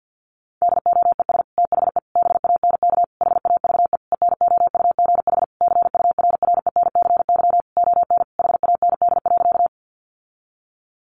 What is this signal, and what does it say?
Morse code, heard as a keyed tone, about 50 wpm, 700 Hz. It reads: DOES THE BANK HAVE ENOUGH CURRENCY ON HAND.
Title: Sentences from Top 1500 Words 50wpm